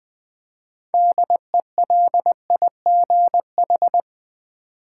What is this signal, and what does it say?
Morse code, heard as a keyed tone, about 20 wpm, 700 Hz. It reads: DELIGH
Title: Build A Word - 3 Letters At A Time - 9-Letter Words - 20wpm